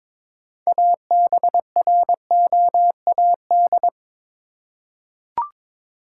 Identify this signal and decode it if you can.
Morse code, heard as a keyed tone, about 22 wpm, 700 Hz. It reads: ABROAD
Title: Six Letter Words - 22wpm